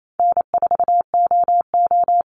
Morse code, heard as a keyed tone, about 28 wpm, 700 Hz. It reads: N4OO
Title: Call Signs Repeated 2x - 28wpm